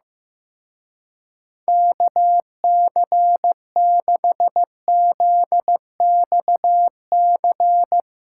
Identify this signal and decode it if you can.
Morse code, heard as a keyed tone, about 18 wpm, 700 Hz. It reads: KC6ZXC
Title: Speed Racing - USA 2x3 Call Signs 15wpm